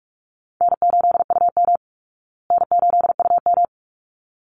Morse code, heard as a keyed tone, about 44 wpm, 700 Hz. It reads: N8UM N8UM
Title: Call Signs Repeated 3x - 45wpm